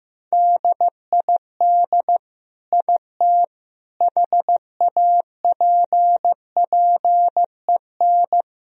Morse code, heard as a keyed tone, about 15 wpm, 700 Hz. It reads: DID IT HAPPEN
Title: Sentences from Top 1000 Words - Review Part 2 - 15wpm